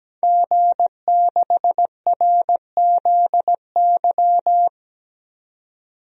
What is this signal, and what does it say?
Morse code, heard as a keyed tone, about 17 wpm, 700 Hz. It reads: G6RZY
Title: DXCC-100 Call Signs - 5 Characters - 17wpm